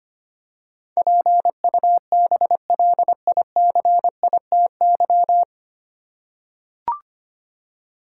Morse code, heard as a keyed tone, about 25 wpm, 700 Hz. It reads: PUBLICITY
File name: Nine Letter Words - 25wpm